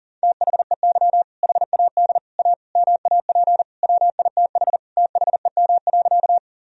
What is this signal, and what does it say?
Morse code, heard as a keyed tone, about 37 wpm, 700 Hz. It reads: THEY HAD A MAP WITH THEM.
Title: Sentences from Top 1000 Words - Review Part 2 - 40wpm